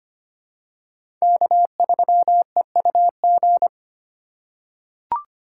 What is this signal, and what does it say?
Morse code, heard as a keyed tone, about 25 wpm, 700 Hz. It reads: K3EUG
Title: USA Call Signs - 5 Characters - 3x Character Spacing - 25wpm